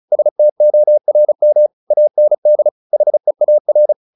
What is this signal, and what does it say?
Morse code, heard as a keyed tone, about 35 wpm, 600 Hz. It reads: STORM AND HEAR
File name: the old man and the sea / the old man and the sea 1 35wpm